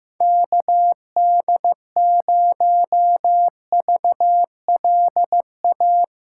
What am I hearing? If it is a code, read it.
Morse code, heard as a keyed tone, about 15 wpm, 700 Hz. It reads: KD0VLA
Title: USA Callsign DE Callsign - 6 Characters - 15wpm